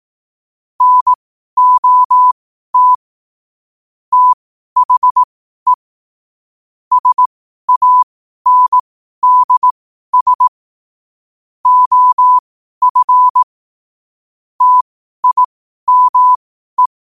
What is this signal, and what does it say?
Morse code, heard as a keyed tone, about 18 wpm, spaced as for 12 wpm, 1000 Hz. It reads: NOT THE SANDS OF TIME
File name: qotdfiles / 12wpm